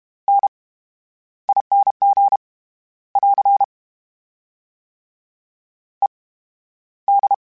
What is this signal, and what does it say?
Morse code, heard as a keyed tone, about 32 wpm, spaced as for 13 wpm, 800 Hz. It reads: NING <AR> ED
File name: Special INGs / Special INGs-mp3s / VerySlowToFaster Ing @30